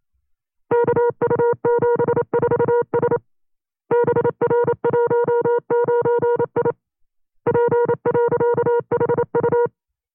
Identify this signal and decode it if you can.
Morse code, heard as a keyed tone, about 29 wpm, 480 Hz. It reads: KU74S BR19I P.HU
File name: FreeFiles / CW-practice / 25WPM-side2